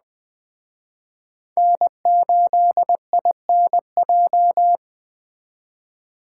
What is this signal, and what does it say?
Morse code, heard as a keyed tone, about 20 wpm, 700 Hz. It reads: N8INJ
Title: Speed Racing - USA Call Signs 17wpm